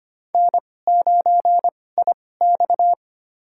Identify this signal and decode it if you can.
Morse code, heard as a keyed tone, about 25 wpm, spaced as for 18 wpm, 700 Hz. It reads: N9IX
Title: USA Call Signs - 2x Character Spacing - 25wpm